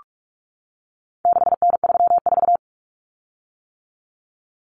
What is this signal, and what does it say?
Morse code, heard as a keyed tone, about 44 wpm, 700 Hz. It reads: 6N34
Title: Common Cut Numbers - 4-Digit 45wpm